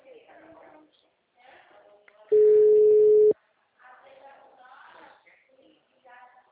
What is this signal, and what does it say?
Ringback tone (Europe, 425 Hz), with speech.